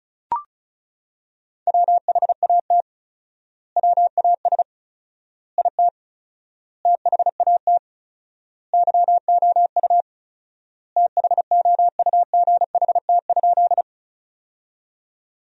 Morse code, heard as a keyed tone, about 36 wpm, 700 Hz. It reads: WHAT WAS IT THAT YOU THOUGHT?
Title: 4x Word-Spacing - Sentences from Top 300 Words - 35wpm